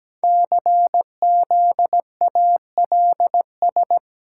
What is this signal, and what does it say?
Morse code, heard as a keyed tone, about 17 wpm, 700 Hz. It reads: CZALS
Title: Code Groups - 5 Letters - 17wpm